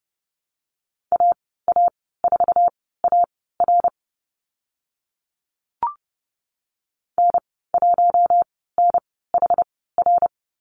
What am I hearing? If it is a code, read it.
Morse code, heard as a keyed tone, about 30 wpm, spaced as for 17 wpm, 700 Hz. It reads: AA4AR N1NHR
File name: Lesson 14g - Number 2 - Call Signs 30wpm